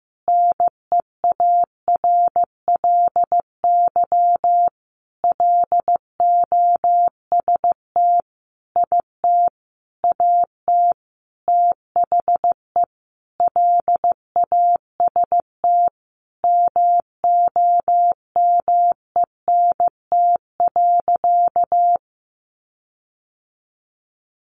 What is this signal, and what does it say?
Morse code, heard as a keyed tone, about 15 wpm, 700 Hz. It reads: NEARLY LOST IT AT THE LAST MOMENT.
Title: Sentences from Top 1400 Words 15wpm